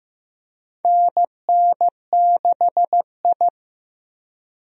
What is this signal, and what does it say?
Morse code, heard as a keyed tone, about 16 wpm, 700 Hz. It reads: NN6I
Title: USA Call Signs - 4 Characters - Repeated 3x - 15wpm